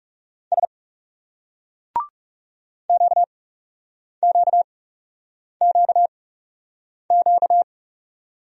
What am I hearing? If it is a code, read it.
Morse code, heard as a keyed tone, about 38 wpm, spaced as for 8 wpm, 700 Hz. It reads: S QQQQ